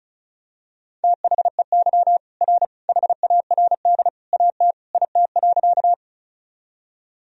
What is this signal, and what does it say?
Morse code, heard as a keyed tone, about 34 wpm, 700 Hz. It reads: THEY R HARD AT IT.
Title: Sentences from Top 300 Words and Abbreviations 35wpm